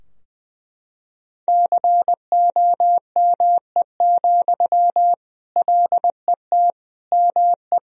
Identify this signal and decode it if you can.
Morse code, heard as a keyed tone, about 21 wpm, 700 Hz. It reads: COME, LET ME
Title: Sentences from Top 1000 Words - Review Part 2 - 20wpm